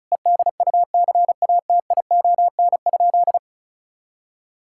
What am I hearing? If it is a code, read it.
Morse code, heard as a keyed tone, about 36 wpm, 700 Hz. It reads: EDUCATION?
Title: Sentences from Top 1000 Words - Review Part 6 - 35wpm